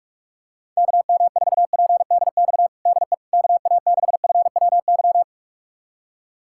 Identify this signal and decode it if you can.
Morse code, heard as a keyed tone, about 47 wpm, 700 Hz. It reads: KM4PDX DE KA6FWY
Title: USA Callsign DE Callsign - 6 Characters - 45wpm